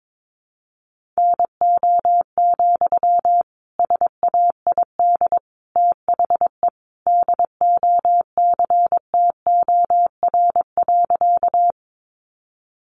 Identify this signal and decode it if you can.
Morse code, heard as a keyed tone, about 22 wpm, 700 Hz. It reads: NO, SAID THE DOCTOR.
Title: Sentences from Top 1500 Words 22wpm